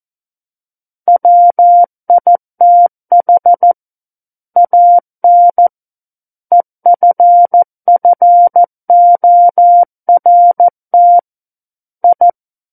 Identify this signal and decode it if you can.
Morse code, heard as a keyed tone, about 14 wpm, 700 Hz. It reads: WITH AN EFFORT I
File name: Book One / 05-Chapter 5 - 14 WPM